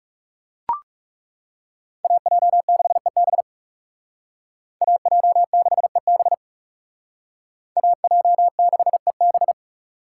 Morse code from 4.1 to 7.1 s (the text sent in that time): AJ6EB